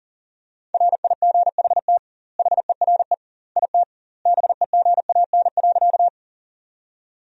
Morse code, heard as a keyed tone, about 37 wpm, 700 Hz. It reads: RIGHT HERE IT BEGAN.
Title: 2x Word-Spacing - Sentences from Top 300 Words - 40wpm